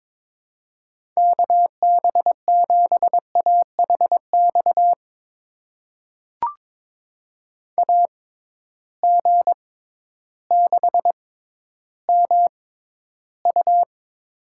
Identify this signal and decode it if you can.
Morse code, heard as a keyed tone, about 22 wpm, spaced as for 12 wpm, 700 Hz. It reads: KB7AHX AG6MU
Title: USA Call Signs - 6x Character Spacing - 22wpm